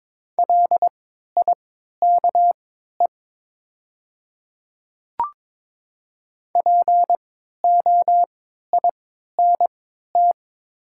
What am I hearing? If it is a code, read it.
Morse code, heard as a keyed tone, about 22 wpm, spaced as for 13 wpm, 690 Hz. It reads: LIKE POINT
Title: 3x Character Spacing - Top 200 Words - 22wpm